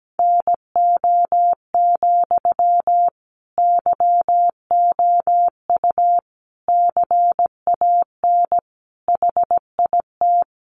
Morse code, heard as a keyed tone, about 17 wpm, 700 Hz. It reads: NO, YOU CAN HIT
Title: Sentences from Top 1200 Words 17wpm